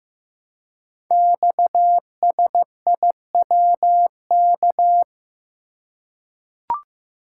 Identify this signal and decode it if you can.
Morse code, heard as a keyed tone, about 15 wpm, 700 Hz. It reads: XSIWK
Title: Code Groups - 5 Letters - 15wpm